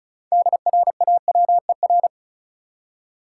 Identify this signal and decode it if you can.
Morse code, heard as a keyed tone, about 34 wpm, 700 Hz. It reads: DRAWER
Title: Six Letter Words - 35wpm